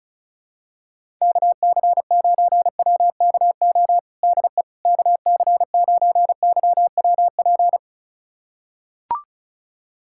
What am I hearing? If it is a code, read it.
Morse code, heard as a keyed tone, about 34 wpm, 700 Hz. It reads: KC9WKO DE KC9YWP
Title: USA Callsign DE Callsign 35wpm